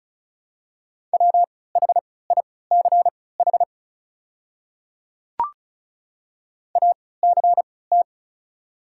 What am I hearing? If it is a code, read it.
Morse code, heard as a keyed tone, about 36 wpm, spaced as for 20 wpm, 700 Hz. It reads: WHICH ACT
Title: 3x Character Spacing - Top 200 Words - 35wpm